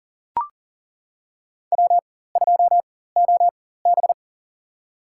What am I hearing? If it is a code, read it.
Morse code, heard as a keyed tone, about 40 wpm, spaced as for 19 wpm, 700 Hz. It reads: W2OB
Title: USA Call Signs - 4 Characters - 4x Character Spacing - 40wpm